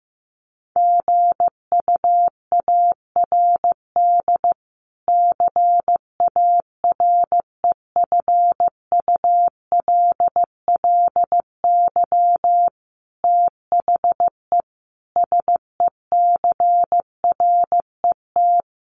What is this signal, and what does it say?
Morse code, heard as a keyed tone, about 14 wpm, 700 Hz. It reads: GUARD CAREFULLY THE SECRET
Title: Sentences from Top 1300 Words 15wpm